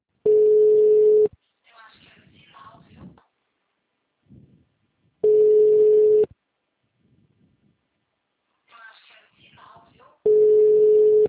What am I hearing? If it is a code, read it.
Ringback tone (Europe, 425 Hz), with speech.
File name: out- -2020- - - .10